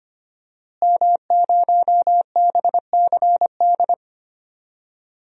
Morse code, heard as a keyed tone, about 25 wpm, 700 Hz. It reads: M0BCD
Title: DXCC-100 Call Signs - 5 Characters - Repeated 3x - 25wpm